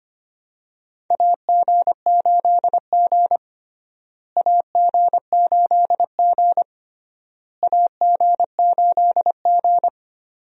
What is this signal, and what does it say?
Morse code, heard as a keyed tone, about 25 wpm, 700 Hz. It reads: AG8G AG8G AG8G
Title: USA Call Signs - 4 Characters - Repeated 3x - 25wpm